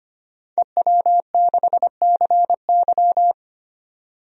Morse code, heard as a keyed tone, about 25 wpm, 700 Hz. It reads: EW6CY
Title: DXCC-100 Call Signs - 5 Characters - 25wpm